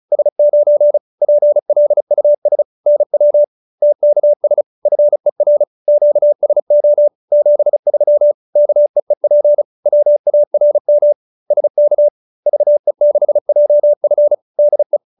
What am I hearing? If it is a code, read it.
Morse code, heard as a keyed tone, about 34 wpm, 590 Hz. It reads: S9 PLUS NW TKS FER QSO 73 KEEP WARM SK VE6JF DE